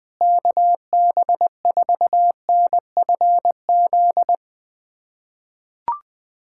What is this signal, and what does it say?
Morse code, heard as a keyed tone, about 20 wpm, 700 Hz. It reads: KB4NFZ
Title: USA Callsign DE Callsign 20wpm